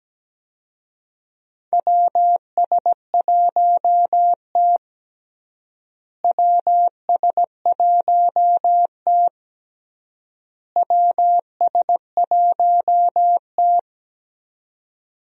Morse code, heard as a keyed tone, about 17 wpm, 700 Hz. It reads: WS1T WS1T WS1T
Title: USA Call Signs - 4 Characters - Repeated 3x - 17wpm